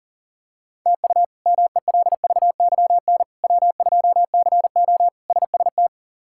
Morse code, heard as a keyed tone, about 40 wpm, 700 Hz. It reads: TU MELVYN W2CO SST
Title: K1USN SST QSO Practice - 40wpm